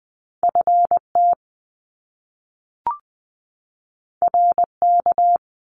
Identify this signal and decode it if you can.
Morse code, heard as a keyed tone, about 20 wpm, 700 Hz. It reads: FT RK
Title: Top Two Letter Combinations in Words 20wpm